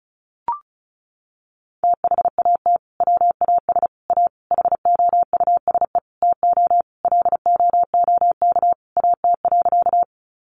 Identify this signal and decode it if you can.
Morse code, heard as a keyed tone, about 34 wpm, 700 Hz. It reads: THAT WAS A HOUSE TO LOOK AT.
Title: Sentences from Top 200 Words 35wpm